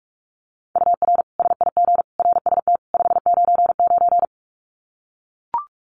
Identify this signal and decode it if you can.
Morse code, heard as a keyed tone, about 44 wpm, 700 Hz. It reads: UR SIG RST 599
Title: Common QSO Phrases 45wpm